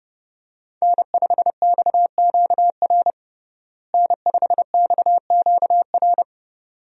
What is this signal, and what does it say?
Morse code, heard as a keyed tone, about 31 wpm, 700 Hz. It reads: N5XQR N5XQR
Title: USA Call Signs - 5 Characters - Repeated 2x - 30wpm